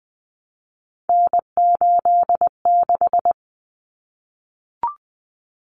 Morse code, heard as a keyed tone, about 20 wpm, 700 Hz. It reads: N86
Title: Common Cut Numbers - 3-Digit 20wpm